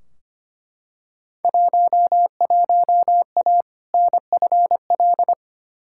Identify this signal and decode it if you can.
Morse code, heard as a keyed tone, about 25 wpm, 700 Hz. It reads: 11A NFL
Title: Field Day Exchanges Encore - 25wpm